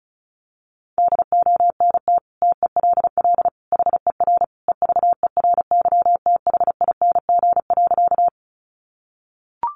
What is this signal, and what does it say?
Morse code, heard as a keyed tone, about 34 wpm, 700 Hz. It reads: DONT TELL HER EVERYTHING.
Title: Sentences from Top 700 Words 35wpm